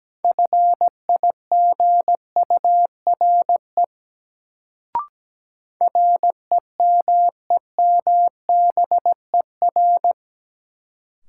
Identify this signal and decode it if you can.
Morse code, heard as a keyed tone, about 17 wpm, 700 Hz. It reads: FIGURE REMEMBER
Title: Sets of 4 Words - New Word Drill - 17wpm